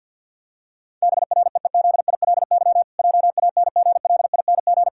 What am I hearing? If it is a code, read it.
Morse code, heard as a keyed tone, about 50 wpm, 700 Hz. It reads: BREEZILY JANGLING
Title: Pangram Challenge v2 - 50wpm